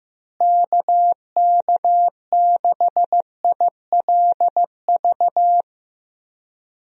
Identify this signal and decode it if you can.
Morse code, heard as a keyed tone, about 15 wpm, 700 Hz. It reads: KK6ILV
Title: USA Call Signs - 6 Characters - Repeated 2x - 15wpm